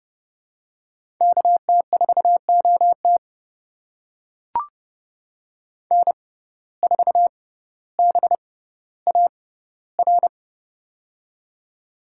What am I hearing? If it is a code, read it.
Morse code, heard as a keyed tone, about 29 wpm, spaced as for 15 wpm, 700 Hz. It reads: KT4OT N4BAR
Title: USA Call Signs - 5 Characters - 6x Character Spacing - 30wpm